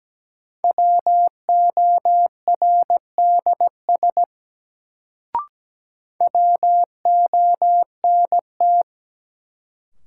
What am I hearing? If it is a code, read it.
Morse code, heard as a keyed tone, about 17 wpm, 700 Hz. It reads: WORDS WONT
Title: Sentences from Top 1000 Words - Review Part 6 - New Word Drill - 17wpm